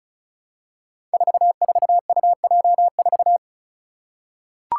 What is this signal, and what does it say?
Morse code, heard as a keyed tone, about 34 wpm, 700 Hz. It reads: 44UJ4
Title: Five Challenging Letter-Number Combinations 35wpm